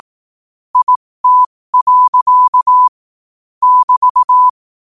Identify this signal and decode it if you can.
Morse code, heard as a keyed tone, about 18 wpm, spaced as for 15 wpm, 1000 Hz. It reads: IT. <BT>
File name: qotdfiles / 15wpm